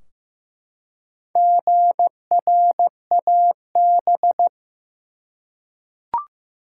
Morse code, heard as a keyed tone, about 15 wpm, 700 Hz. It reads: GRAB
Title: Four Letter Words - 15wpm